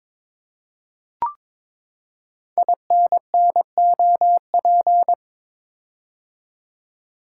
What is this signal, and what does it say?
Morse code, heard as a keyed tone, about 22 wpm, 700 Hz. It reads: INNOP
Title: Code Groups - 5 Letters - 22wpm